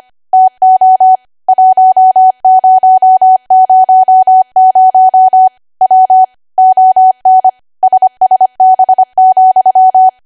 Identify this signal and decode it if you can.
Morse code, heard as a keyed tone, about 25 wpm, 750 Hz. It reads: TO 1000 W ON SSB,
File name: Archive / 25 WPM / 25WPM